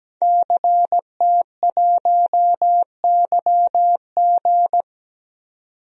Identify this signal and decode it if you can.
Morse code, heard as a keyed tone, about 17 wpm, 700 Hz. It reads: CT1YG
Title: DXCC-100 Call Signs - 5 Characters - 17wpm